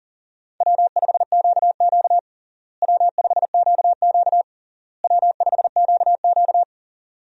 Morse code, heard as a keyed tone, about 37 wpm, 700 Hz. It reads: W5QQ W5QQ W5QQ
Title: USA Call Signs - 4 Characters - Repeated 3x - 40wpm